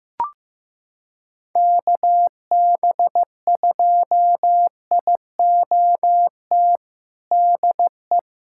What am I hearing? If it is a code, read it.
Morse code, heard as a keyed tone, about 15 wpm, 700 Hz. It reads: KB2IOT DE
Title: USA Callsign DE Callsign - 6 Characters - 15wpm